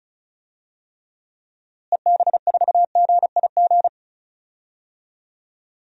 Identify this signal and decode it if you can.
Morse code, heard as a keyed tone, about 36 wpm, 700 Hz. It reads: EB4GIG
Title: DXCC-100 Call Signs - 6 Characters - Repeated 2x - 35wpm